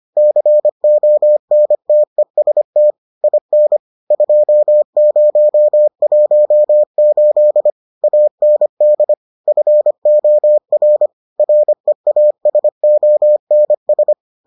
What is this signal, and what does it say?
Morse code, heard as a keyed tone, about 25 wpm, 600 Hz. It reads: CONTEST IN 2018 AND FOR REASONS